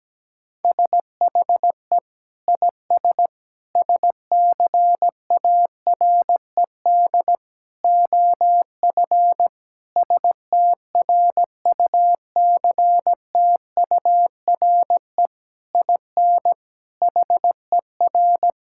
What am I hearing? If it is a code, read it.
Morse code, heard as a keyed tone, about 17 wpm, 700 Hz. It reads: SHE IS SCARED OF STRUCTURE IN HER
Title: Sentences from Top 1000 Words - Review Part 6 - 17wpm